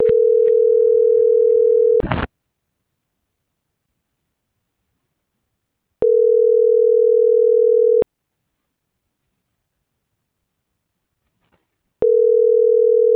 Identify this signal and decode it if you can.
Ringback tone (North America), 440 + 480 Hz.